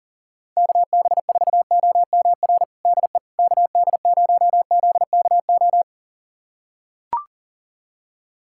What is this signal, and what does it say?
Morse code, heard as a keyed tone, about 41 wpm, 690 Hz. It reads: KB4OMR DE KD0ZKO
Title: USA Callsign DE Callsign - 6 Characters - 40wpm